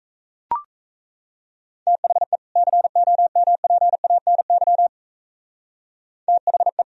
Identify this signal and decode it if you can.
Morse code, heard as a keyed tone, about 33 wpm, 700 Hz. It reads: THE COMPANY THE
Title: Speed-Racing - Sets of 2 Words 28wpm